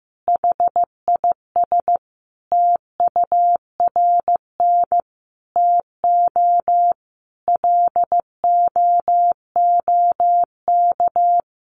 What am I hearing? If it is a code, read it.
Morse code, heard as a keyed tone, about 15 wpm, 700 Hz. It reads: HIS TURN TO LOOK
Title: Sentences from Top 200 Words 15wpm